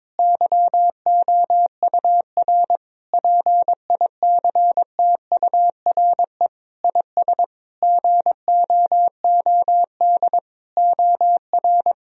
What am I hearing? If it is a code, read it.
Morse code, heard as a keyed tone, about 22 wpm, 700 Hz. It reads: YOUR PICTURE IS GOOD OR
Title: Sentences from Top 1000 Words - Review Part 1 - 22wpm